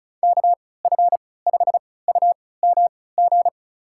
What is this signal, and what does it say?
Morse code, heard as a keyed tone, about 36 wpm, spaced as for 20 wpm, 700 Hz. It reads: KF5UMG
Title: USA Call Signs - 6 Characters - 3x Character Spacing - 35wpm